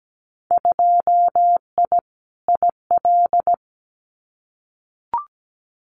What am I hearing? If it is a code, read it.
Morse code, heard as a keyed tone, about 17 wpm, 700 Hz. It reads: 2I IL
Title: Winter Field Day Exchanges Encore 17wpm